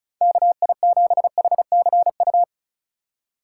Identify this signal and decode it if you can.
Morse code, heard as a keyed tone, about 36 wpm, 700 Hz. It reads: KI7HCU
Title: USA Call Signs - 6 Characters - 2x Character Spacing - 35wpm